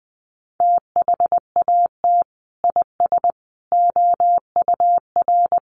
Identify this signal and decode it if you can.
Morse code, heard as a keyed tone, about 20 wpm, 700 Hz. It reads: THAT IS OUR
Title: Sentences from Top 1400 Words 20wpm